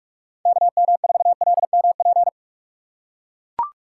Morse code, heard as a keyed tone, about 44 wpm, 700 Hz. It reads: KM4LMP
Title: USA Call Signs - 6 Characters - 4x Character Spacing - 45wpm